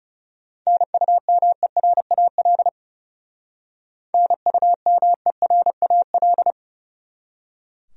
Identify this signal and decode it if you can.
Morse code, heard as a keyed tone, about 36 wpm, 700 Hz. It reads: NUMERAL NUMERAL
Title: Speed-Racing Top 500 Words 30wpm